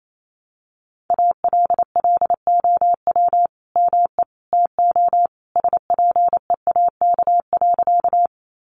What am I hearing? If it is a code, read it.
Morse code, heard as a keyed tone, about 28 wpm, 700 Hz. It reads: ALLOW ME TO SPEAK.
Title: Sentences from Top 1100 Words 28wpm